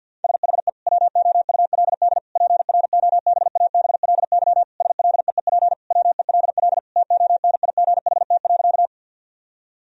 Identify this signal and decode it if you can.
Morse code, heard as a keyed tone, about 50 wpm, 700 Hz. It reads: SHE WOULD PROBABLY SLEEP WELL TONIGHT.
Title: Sentences from Top 1000 Words - Review Part 4 - 50wpm